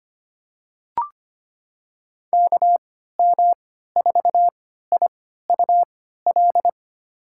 Morse code, heard as a keyed tone, about 25 wpm, spaced as for 14 wpm, 700 Hz. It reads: KM4IUL
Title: USA Call Signs - 6 Characters - 3x Character Spacing - 25wpm